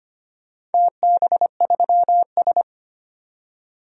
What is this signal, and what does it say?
Morse code, heard as a keyed tone, about 25 wpm, 700 Hz. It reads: TB3S
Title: DXCC-100 Call Signs - 4 Characters - 25wpm